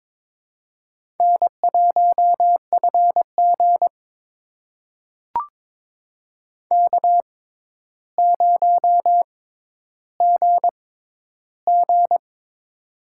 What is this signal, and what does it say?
Morse code, heard as a keyed tone, about 22 wpm, spaced as for 11 wpm, 700 Hz. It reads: N1FG K0GG
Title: USA Call Signs - 4 Characters - 6x Character Spacing - 22wpm